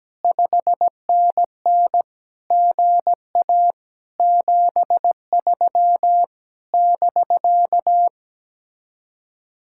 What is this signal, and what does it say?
Morse code, heard as a keyed tone, about 17 wpm, 700 Hz. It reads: 5NN GA 73 <BK>
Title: Speed-Racing Common QSO Phrases Encore - 17wpm